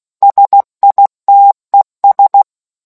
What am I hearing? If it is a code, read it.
Morse code, heard as a keyed tone, about 16 wpm, 800 Hz. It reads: SITES